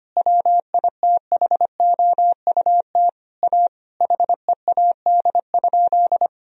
Morse code, heard as a keyed tone, about 25 wpm, 700 Hz. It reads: WITHOUT A HEAD?
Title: Sentences from Top 1000 Words - Review Part 3 - 25wpm